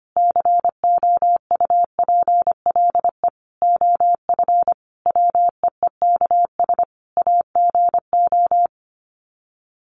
Morse code, heard as a keyed tone, about 25 wpm, 700 Hz. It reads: COUPLE OF WEEKS AGO
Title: Sets of 5 Words 25wpm